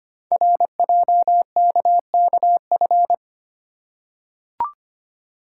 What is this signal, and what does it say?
Morse code, heard as a keyed tone, about 25 wpm, 700 Hz. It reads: RJKKF
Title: Code Groups - 5 Letters - 25wpm